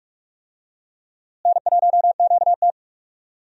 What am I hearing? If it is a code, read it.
Morse code, heard as a keyed tone, about 44 wpm, 700 Hz. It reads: N1QT
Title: USA Call Signs - 4 Characters - Repeated 2x - 45wpm